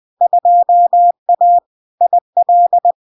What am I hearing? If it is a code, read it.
Morse code, heard as a keyed tone, about 20 wpm, 700 Hz. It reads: 2A IL
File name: LARK-FD-2022-Log-in-CW-20wpm-700-Hz